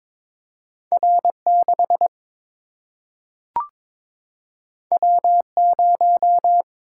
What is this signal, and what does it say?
Morse code, heard as a keyed tone, about 22 wpm, 700 Hz. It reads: R 6 W 0
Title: Single-Letter Single-Number 22wpm